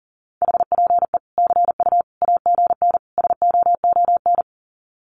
Morse code, heard as a keyed tone, about 40 wpm, 700 Hz. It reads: HPE CU AGN SOON
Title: Common QSO Phrases 40wpm